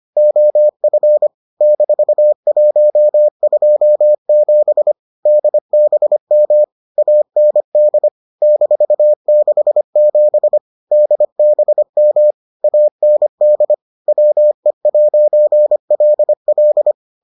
Morse code, heard as a keyed tone, about 25 wpm, 600 Hz. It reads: OF -127 DBM AND -67 DBM AND WE'LL